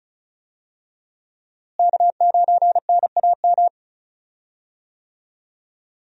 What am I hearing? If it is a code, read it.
Morse code, heard as a keyed tone, about 36 wpm, 700 Hz. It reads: K9NAM
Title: USA Call Signs - 5 Characters - Repeated 3x - 35wpm